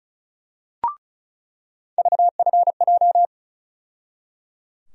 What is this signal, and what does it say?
Morse code, heard as a keyed tone, about 34 wpm, 700 Hz. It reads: VFJ
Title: USA Call Sign Postfixes - 3 Letters 35wpm